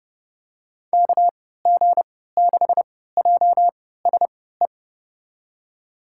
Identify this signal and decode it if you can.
Morse code, heard as a keyed tone, about 29 wpm, spaced as for 17 wpm, 700 Hz. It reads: KG6JSE
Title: USA Call Signs - 6 Characters - 3x Character Spacing - 30wpm